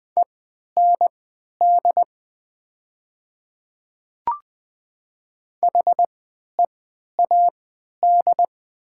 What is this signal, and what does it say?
Morse code, heard as a keyed tone, about 20 wpm, spaced as for 11 wpm, 700 Hz. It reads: END HEAD
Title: 3x Character Spacing - Top 500 Words - 20wpm